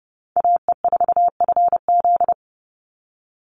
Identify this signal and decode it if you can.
Morse code, heard as a keyed tone, about 29 wpm, 700 Hz. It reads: AE4FZ
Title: NA SOTA Chaser Callsigns 30wpm